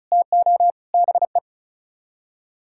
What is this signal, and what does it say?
Morse code, heard as a keyed tone, about 45 wpm, 700 Hz. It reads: TO BE